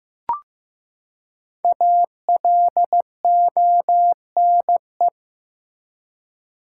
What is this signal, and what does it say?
Morse code, heard as a keyed tone, about 15 wpm, 700 Hz. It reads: ALONE
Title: Five Letter Words - 15wpm